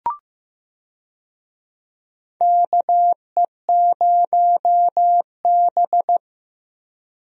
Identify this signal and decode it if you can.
Morse code, heard as a keyed tone, about 15 wpm, 700 Hz. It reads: KE0B
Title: USA Call Signs - 4 Characters - Repeated 3x - 15wpm